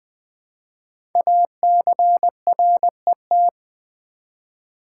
Morse code, heard as a keyed tone, about 20 wpm, 700 Hz. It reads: ACRET
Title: Code Groups - 5 Letters - 20wpm